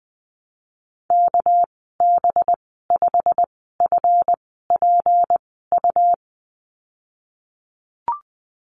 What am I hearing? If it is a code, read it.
Morse code, heard as a keyed tone, about 20 wpm, spaced as for 15 wpm, 700 Hz. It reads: KB5FPU
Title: 200 Lesson 32g - Letter k - Call Signs 20wpm